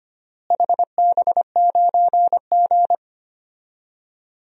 Morse code, heard as a keyed tone, about 25 wpm, 700 Hz. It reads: HB9G
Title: DXCC-100 Call Signs - 4 Characters - Repeated 3x - 25wpm